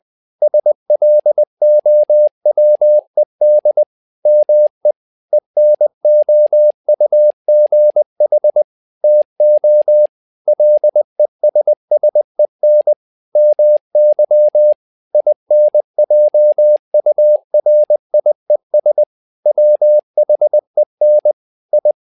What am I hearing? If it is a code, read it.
Morse code, heard as a keyed tone, about 20 wpm, 600 Hz. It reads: SLOWED ME ENOUGH TO LESSEN MY INJURIES WHEN I
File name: tower trouble 20wpm